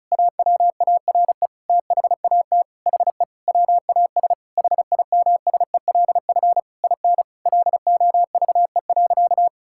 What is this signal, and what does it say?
Morse code, heard as a keyed tone, about 36 wpm, 700 Hz. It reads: AWARE THAT HE WAS HIMSELF IN LOVE.
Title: Sentences from Top 1600 Words 35wpm